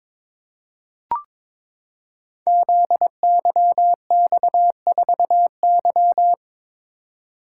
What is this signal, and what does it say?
Morse code, heard as a keyed tone, about 22 wpm, 700 Hz. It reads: ZYX4Y
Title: Code Groups - 5 Letter:Numbers - 22wpm